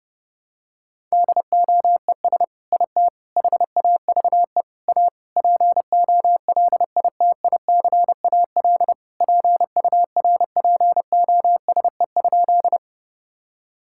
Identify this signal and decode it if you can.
Morse code, heard as a keyed tone, about 30 wpm, 700 Hz. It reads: DOES IT HAVE A POLITICAL PURPOSE?
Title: Sentences from Top 1000 Words - Review Part 5 - 30wpm